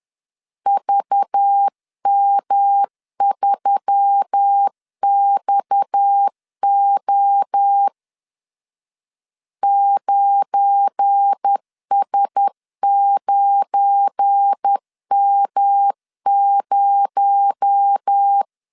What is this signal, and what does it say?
Morse code, heard as a keyed tone, about 10 wpm, 800 Hz. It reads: VM3XO 9S9M0
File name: LEZIONI / 9 286 10wpm 2735